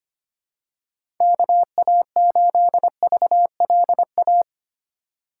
Morse code, heard as a keyed tone, about 25 wpm, 700 Hz. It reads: KA8VLA
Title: USA Call Signs - 6 Characters - 4x Character Spacing - 25wpm